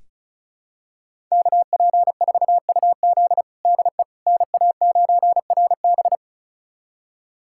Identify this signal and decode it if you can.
Morse code, heard as a keyed tone, about 36 wpm, 690 Hz. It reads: KP4UZ DE NA9RB
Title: USA Callsign DE Callsign - 5 Characters - 35wpm